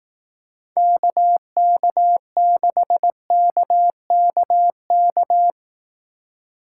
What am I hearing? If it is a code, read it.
Morse code, heard as a keyed tone, about 18 wpm, 700 Hz. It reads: KK6KKK
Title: Speed Racing - USA 2x3 Call Signs 15wpm